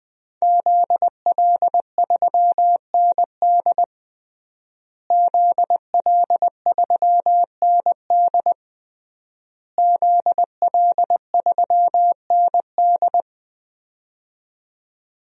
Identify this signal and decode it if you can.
Morse code, heard as a keyed tone, about 20 wpm, 700 Hz. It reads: ZL3ND ZL3ND ZL3ND
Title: DXCC-100 Call Signs - 5 Characters - Repeated 3x - 20wpm